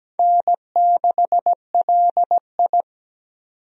Morse code, heard as a keyed tone, about 17 wpm, 700 Hz. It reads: N6LI
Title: USA Call Signs - 4 Characters - 17wpm